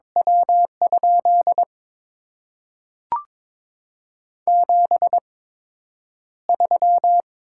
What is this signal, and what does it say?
Morse code, heard as a keyed tone, about 22 wpm, spaced as for 9 wpm, 690 Hz. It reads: W? 73
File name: Top 90 QSO Elements - 8x Character Spacing - 22wpm